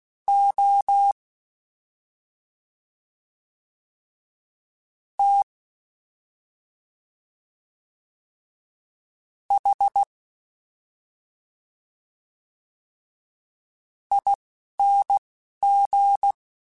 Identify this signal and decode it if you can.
Morse code, heard as a keyed tone, about 16 wpm, spaced as for 3 wpm, 800 Hz. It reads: OTHING
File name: Special INGs-mp3s / Special INGs-mp3s / Slow ing @10